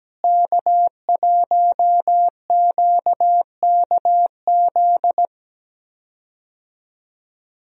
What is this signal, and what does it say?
Morse code, heard as a keyed tone, about 17 wpm, 700 Hz. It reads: K1QKZ
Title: USA Call Signs - 5 Characters - Repeated 2x - 17wpm